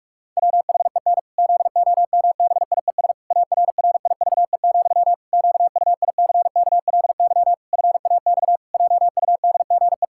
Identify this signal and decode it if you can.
Morse code, heard as a keyed tone, about 46 wpm, 700 Hz. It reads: WHEN ZOMBIES ARRIVE, QUICKLY FAX JUDGE
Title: Pangram Challenge v2 - 45wpm